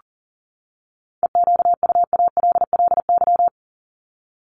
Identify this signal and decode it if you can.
Morse code, heard as a keyed tone, about 40 wpm, 700 Hz. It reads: EQUALLY
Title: LY Suffix - Top 100 Words with Base Words 40wpm